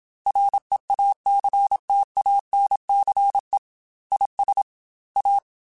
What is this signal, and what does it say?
Morse code, heard as a keyed tone, about 27 wpm, 800 Hz. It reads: REACTANCE IS A
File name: Technical Subjects / Technical Subjects-mp3s / Reactance @25